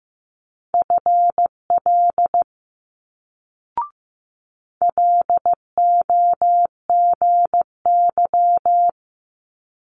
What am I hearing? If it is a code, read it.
Morse code, heard as a keyed tone, about 15 wpm, 700 Hz. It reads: FL LOGY
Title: Multi-letter Phonemes Quick Fire 15wpm